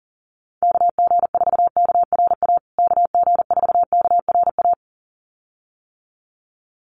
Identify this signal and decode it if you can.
Morse code, heard as a keyed tone, about 40 wpm, 700 Hz. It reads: KG4KRA KG4KRA
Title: USA Call Signs Repeated 2x 40wpm